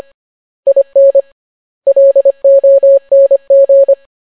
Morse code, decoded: IN LONG